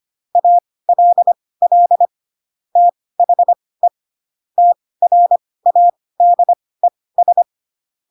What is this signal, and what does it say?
Morse code, heard as a keyed tone, about 25 wpm, spaced as for 18 wpm, 700 Hz. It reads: ALL THE TRADES
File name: Pinocchio / Pinocchio ch4 12 25 ik1hge